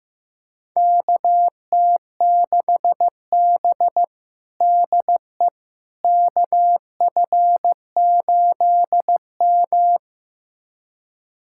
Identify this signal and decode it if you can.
Morse code, heard as a keyed tone, about 15 wpm, 700 Hz. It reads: KT6B DE KF8M
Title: USA Callsign DE Callsign - 4 Characters - 15wpm